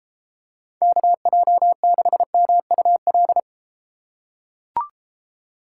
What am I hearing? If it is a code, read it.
Morse code, heard as a keyed tone, about 29 wpm, 700 Hz. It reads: KJ6MUL